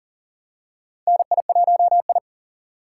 Morse code, heard as a keyed tone, about 37 wpm, 700 Hz. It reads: NI1I
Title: USA Call Signs - 4 Characters - 3x Character Spacing - 40wpm